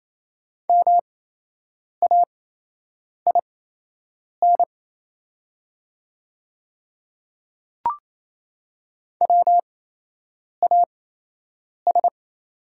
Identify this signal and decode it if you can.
Morse code, heard as a keyed tone, about 28 wpm, spaced as for 8 wpm, 700 Hz. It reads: MAIN WAS
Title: 8x Character Spacing - Top 300 Words - 28wpm